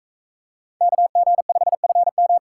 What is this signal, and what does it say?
Morse code, heard as a keyed tone, about 42 wpm, 700 Hz. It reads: KG5FM